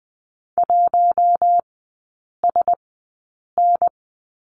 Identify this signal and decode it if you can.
Morse code, heard as a keyed tone, about 20 wpm, spaced as for 8 wpm, 690 Hz. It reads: 1SN
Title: 093 Lesson 16e - Letter c - Three Characters 20wpm